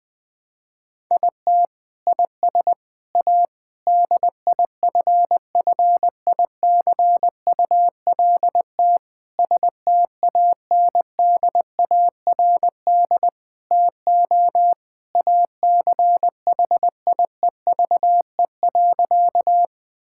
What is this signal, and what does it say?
Morse code, heard as a keyed tone, about 20 wpm, 700 Hz. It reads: IT IS A DIFFICULT STANDARD TO ACHIEVE.
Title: Sentences from Top 1000 Words - Review Part 6 - 20wpm